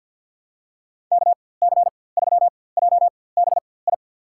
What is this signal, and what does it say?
Morse code, heard as a keyed tone, about 46 wpm, spaced as for 23 wpm, 700 Hz. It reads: KC3JBI
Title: USA Call Signs - 6 Characters - 4x Character Spacing - 50wpm